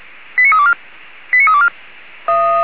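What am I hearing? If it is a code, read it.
Five-tone selective call (ZVEI), heard as quick runs of five tones, tones of about 70 ms. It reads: 89324 89324
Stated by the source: ZVEI five-tone call 89327 received by SDR.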